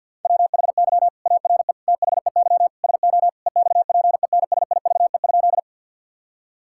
Morse code, heard as a keyed tone, about 50 wpm, 700 Hz. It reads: WHY ARE THEY SO EXPENSIVE?
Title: Sentences from Top 1000 Words - Review Part 6 - 50wpm